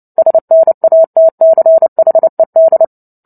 Morse code, heard as a keyed tone, about 30 wpm, 650 Hz. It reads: SNATCHED
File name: Book Two / 03-Chapter 3 - 29 WPM